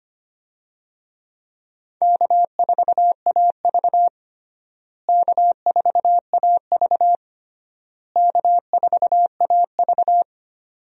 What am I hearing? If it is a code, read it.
Morse code, heard as a keyed tone, about 25 wpm, 700 Hz. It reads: K4AV K4AV K4AV
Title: USA Call Signs - 4 Characters - Repeated 3x - 25wpm